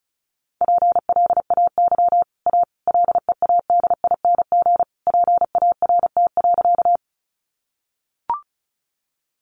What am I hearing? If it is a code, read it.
Morse code, heard as a keyed tone, about 34 wpm, 700 Hz. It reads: PLAY A LEADING PART.
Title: Sentences from Top 1100 Words 35wpm